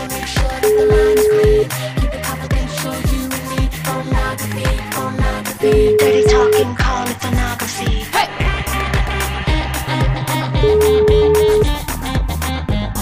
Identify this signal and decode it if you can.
Ringback tone (Europe, 425 Hz), with noise.